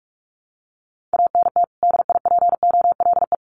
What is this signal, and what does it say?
Morse code, heard as a keyed tone, about 44 wpm, 700 Hz. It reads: ANT DIPOLE
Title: Common QSO Phrases 45wpm